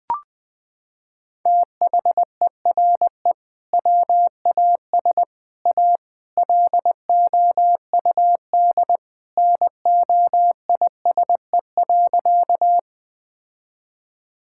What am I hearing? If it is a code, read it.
Morse code, heard as a keyed tone, about 20 wpm, 700 Hz. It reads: THERE WAS A LOUD NOISE.
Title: Sentences with 3-Word Phrases - 20wpm